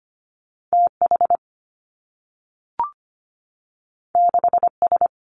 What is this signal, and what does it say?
Morse code, heard as a keyed tone, about 25 wpm, 700 Hz. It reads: T H 6 S
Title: Two Challenging Letter-Number Combinations 25wpm